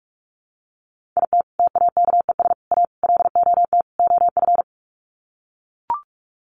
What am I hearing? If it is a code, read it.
Morse code, heard as a keyed tone, about 44 wpm, 700 Hz. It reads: IT TAKES A LOT OF
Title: Sets of 5 Words Encore 45wpm